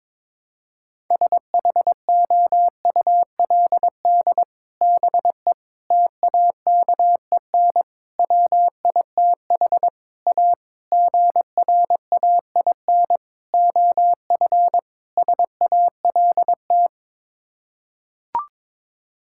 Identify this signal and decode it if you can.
Morse code, heard as a keyed tone, about 22 wpm, 700 Hz. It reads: SHOULD BE TAKEN WITH A GRAIN OF SALT
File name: practice / 22wpm / Sets of 8 Words - 22wpm